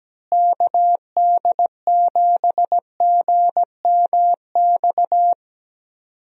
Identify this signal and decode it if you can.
Morse code, heard as a keyed tone, about 17 wpm, 700 Hz. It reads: KD7GMX
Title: USA Call Signs - 6 Characters - Repeated 3x - 17wpm